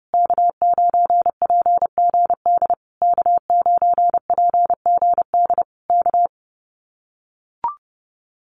Morse code, heard as a keyed tone, about 30 wpm, 700 Hz. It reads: K9PGD K9PGD K
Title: Common QSO Phrases Encore 30wpm